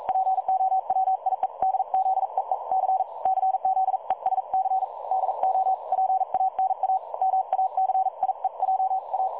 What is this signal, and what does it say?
Morse code, heard as a keyed tone, about 42 wpm, 760 Hz. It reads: POWIEDZEE C CZESC DO PANA MACIEJA
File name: dlazaawansowanych / nagraniazpasma / websdr recording 3565.0kHz